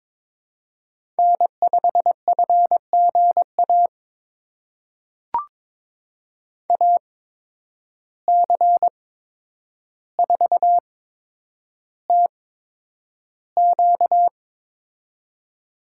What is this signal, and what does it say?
Morse code, heard as a keyed tone, about 22 wpm, spaced as for 9 wpm, 700 Hz. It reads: N5FGA AC4TQ
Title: USA Call Signs - 5 Characters - 8x Character Spacing - 22wpm